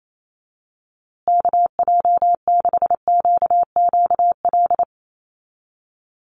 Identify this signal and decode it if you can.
Morse code, heard as a keyed tone, about 28 wpm, 700 Hz. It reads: KJ6QQL
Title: USA Call Signs Repeated 3x 28wpm